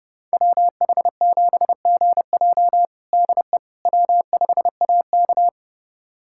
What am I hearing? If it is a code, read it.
Morse code, heard as a keyed tone, about 30 wpm, 700 Hz. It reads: WH7GJ DE W5AK
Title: USA Callsign DE Callsign 30wpm